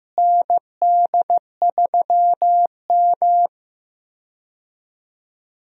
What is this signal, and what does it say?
Morse code, heard as a keyed tone, about 15 wpm, 700 Hz. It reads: ND3M
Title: USA Call Signs - 4 Characters - Repeated 3x - 15wpm